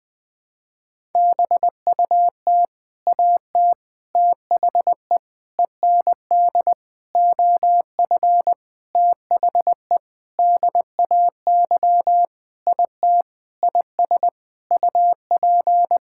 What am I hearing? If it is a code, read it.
Morse code, heard as a keyed tone, about 20 wpm, 700 Hz. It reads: BUT AT THE END OF THE DAY IT IS UP
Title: Sentences with 7-Word Phrases - 20wpm